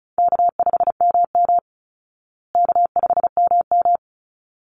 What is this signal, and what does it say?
Morse code, heard as a keyed tone, about 34 wpm, 700 Hz. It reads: K5MM K5MM
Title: Easy Call Signs Repeated 2x - 35wpm